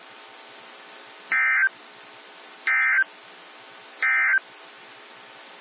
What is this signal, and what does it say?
Emergency Alert System (SAME) tones, heard as harsh buzzing bursts. It is heard from 1.2 to 4.5 s.
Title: NOAA Weather Radio - WXJ73